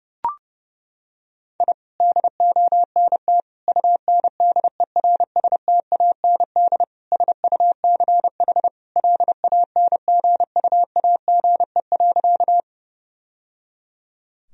Morse code, heard as a keyed tone, about 32 wpm, 700 Hz. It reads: I DONT UNDERSTAND SUCH LANGUAGE.
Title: Sentences from Top 1000 Words - Review Part 3 - 30wpm